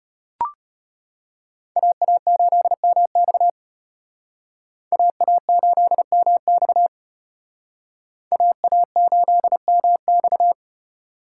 Morse code, heard as a keyed tone, about 32 wpm, 700 Hz. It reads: AA8MX AA8MX AA8MX